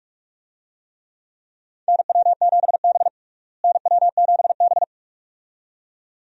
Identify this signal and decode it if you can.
Morse code, heard as a keyed tone, about 42 wpm, 700 Hz. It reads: NW7B NW7B
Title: USA Call Signs - 4 Characters - Repeated 2x - 45wpm